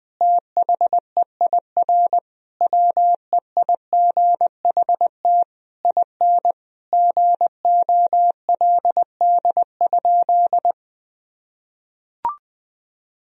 Morse code, heard as a keyed tone, about 20 wpm, 700 Hz. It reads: THEIR WEIGHT IN GOLD?
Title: Sentences from Top 1000 Words - Review Part 3 - 20wpm